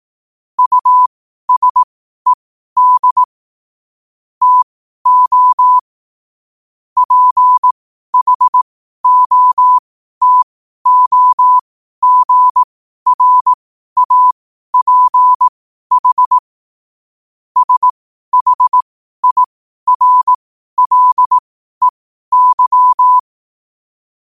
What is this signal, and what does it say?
Morse code, heard as a keyed tone, about 18 wpm, spaced as for 12 wpm, 1000 Hz. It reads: USED TO PHOTOGRAPH SHIRLEY
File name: qotdfiles / 12wpm